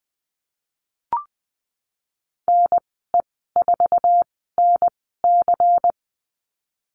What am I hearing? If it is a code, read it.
Morse code, heard as a keyed tone, about 20 wpm, spaced as for 15 wpm, 700 Hz. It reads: NE4NC
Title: 147 Lesson 24g - Letter y - Call Signs 20wpm